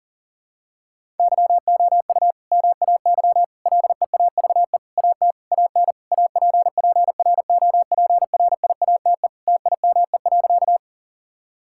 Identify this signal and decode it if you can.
Morse code, heard as a keyed tone, about 40 wpm, 700 Hz. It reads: YOU MAY LEAVE AT AN APPROPRIATE TIME.
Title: Sentences from Top 1000 Words - Review Part 5 - 40wpm